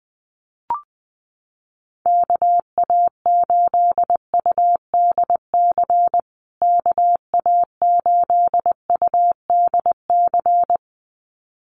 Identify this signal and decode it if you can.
Morse code, heard as a keyed tone, about 20 wpm, 700 Hz. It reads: KA8UDC KA8UDC
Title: USA Call Signs Repeated 2x 20wpm